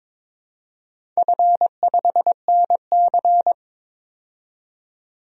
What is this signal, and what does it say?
Morse code, heard as a keyed tone, about 22 wpm, 700 Hz. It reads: F5NC
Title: DXCC-100 Call Signs - 4 Characters - 22wpm